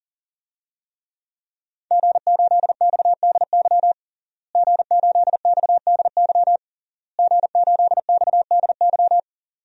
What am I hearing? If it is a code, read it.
Morse code, heard as a keyed tone, about 39 wpm, 700 Hz. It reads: G8XDY G8XDY G8XDY
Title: DXCC-100 Call Signs - 5 Characters - Repeated 3x - 40wpm